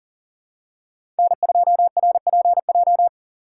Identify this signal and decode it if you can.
Morse code, heard as a keyed tone, about 40 wpm, 700 Hz. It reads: N2RPJ
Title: USA Call Signs - 5 Characters - 4x Character Spacing - 40wpm